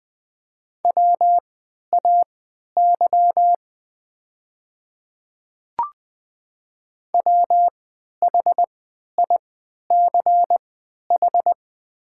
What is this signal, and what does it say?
Morse code, heard as a keyed tone, about 20 wpm, spaced as for 11 wpm, 700 Hz. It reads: WAY WHICH
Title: 3x Character Spacing - Top 100 Words - 20wpm